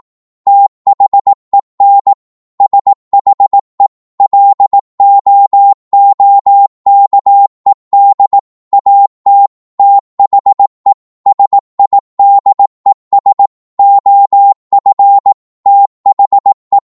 Morse code, heard as a keyed tone, about 18 wpm, 800 Hz. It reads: THEN SHE LOOKED AT THE SIDES OF THE
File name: Morse / psg-18wpm